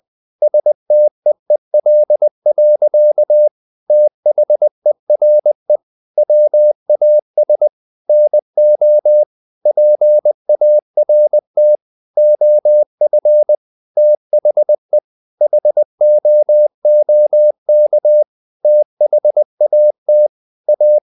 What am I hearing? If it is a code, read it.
Morse code, heard as a keyed tone, about 20 wpm, 600 Hz. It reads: STEEL. THERE WAS NO PART OF THE HOOK THAT A
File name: the old man and the sea / the old man and the sea part 1 20wpm